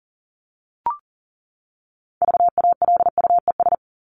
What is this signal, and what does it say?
Morse code, heard as a keyed tone, about 40 wpm, 700 Hz. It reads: VALUES
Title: ES:S Suffix - Top 100 Words with Base Words 40wpm